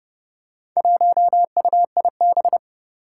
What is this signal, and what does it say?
Morse code, heard as a keyed tone, about 30 wpm, 700 Hz. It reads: 1UIB
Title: Four Challenging Letter-Number Combinations 30wpm